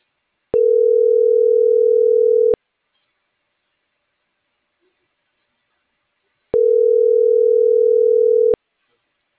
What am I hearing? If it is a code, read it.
Ringback tone (North America), 440 + 480 Hz.